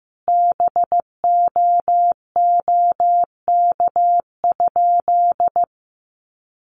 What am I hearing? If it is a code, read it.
Morse code, heard as a keyed tone, about 15 wpm, 700 Hz. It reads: BOOK?
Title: Sentences from Top 1200 Words 15wpm